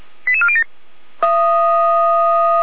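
Five-tone selective call (ZVEI), heard as quick runs of five tones, tones of about 70 ms. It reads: 88407
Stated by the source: ZVEI five-tone selective call received by SDR, fire brigade Feuerwehr Orlamünde.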